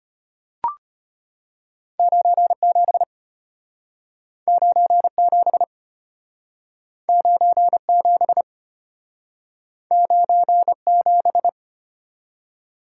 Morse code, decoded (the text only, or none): 97 97 97 97